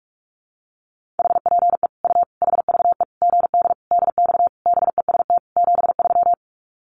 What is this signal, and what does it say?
Morse code, heard as a keyed tone, about 44 wpm, 700 Hz. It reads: HPE U HVE GD DX BEST 73
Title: Common QSO Phrases 45wpm